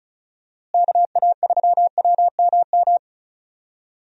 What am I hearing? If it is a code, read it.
Morse code, heard as a keyed tone, about 36 wpm, 700 Hz. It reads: KA3WMM
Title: USA Call Signs - 6 Characters - Repeated 2x - 35wpm